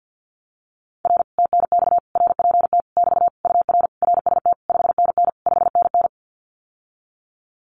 Morse code, heard as a keyed tone, about 50 wpm, 700 Hz. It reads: R TNX RPT <BT> UR RST 5NN 5NN
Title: Common QSO Phrases Encore 50wpm